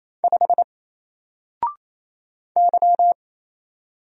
Morse code, heard as a keyed tone, about 27 wpm, 700 Hz. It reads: 5 Y
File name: Single-Letter-Number-28wpm